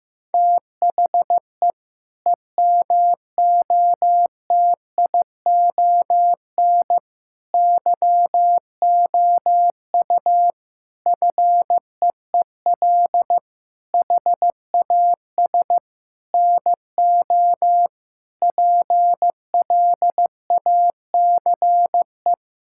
Morse code, heard as a keyed tone, about 15 wpm, 700 Hz. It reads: THE EMOTION YOU FEEL HAS NO PLACE
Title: Sentences from Top 1600 Words 15wpm